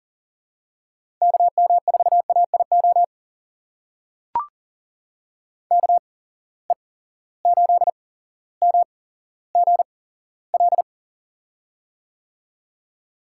Morse code, heard as a keyed tone, about 38 wpm, spaced as for 17 wpm, 700 Hz. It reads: KM4AIO KE8MGL
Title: USA Call Signs - 6 Characters - 8x Character Spacing - 40wpm